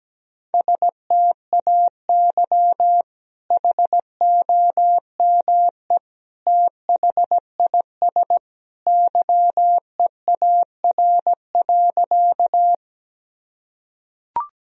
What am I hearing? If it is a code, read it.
Morse code, heard as a keyed tone, about 17 wpm, 700 Hz. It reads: STAY HOME THIS YEAR.
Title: Sentences from Top 1000 Words - Review Part 1 - 17wpm